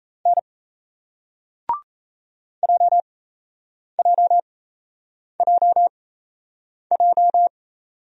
Morse code, heard as a keyed tone, about 37 wpm, spaced as for 8 wpm, 700 Hz. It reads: N JJJJ